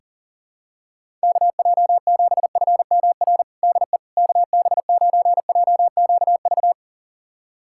Morse code, heard as a keyed tone, about 40 wpm, 700 Hz. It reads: KJ7FMR DE KB9JQV
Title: USA Callsign DE Callsign 40wpm